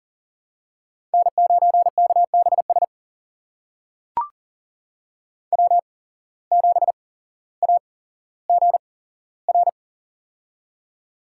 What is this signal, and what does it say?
Morse code, heard as a keyed tone, about 39 wpm, spaced as for 17 wpm, 700 Hz. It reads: N9KBS W7AGR
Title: USA Call Signs - 5 Characters - 8x Character Spacing - 40wpm